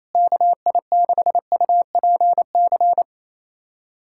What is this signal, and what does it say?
Morse code, heard as a keyed tone, about 28 wpm, 700 Hz. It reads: KI6UPC
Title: USA Callsign DE Callsign 28wpm